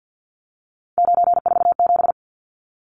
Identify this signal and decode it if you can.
Morse code, heard as a keyed tone, about 50 wpm, 700 Hz. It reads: 947
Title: Numbers - 3-Digit 50wpm